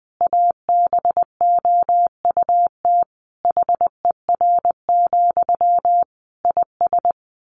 Morse code, heard as a keyed tone, about 19 wpm, 700 Hz. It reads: ABOUT HER, IS
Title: Sentences from Top 1500 Words 20wpm